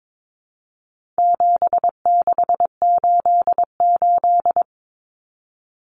Morse code, heard as a keyed tone, about 22 wpm, 700 Hz. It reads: 7688
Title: Common Cut Numbers - 4-Digit 22wpm